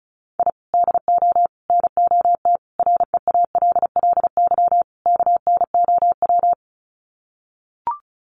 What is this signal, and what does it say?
Morse code, heard as a keyed tone, about 34 wpm, 700 Hz. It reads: I DO NOT REALLY KNOW
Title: Sets of 5 Words Encore 35wpm